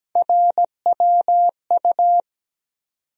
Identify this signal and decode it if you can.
Morse code, heard as a keyed tone, about 17 wpm, 700 Hz. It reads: RWU
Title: USA Call Sign Postfixes - 3 Letters 17wpm